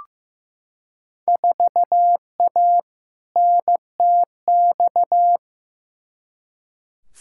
Morse code, heard as a keyed tone, about 15 wpm, 700 Hz. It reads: 4A NTX
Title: Field Day Exchanges - 15wpm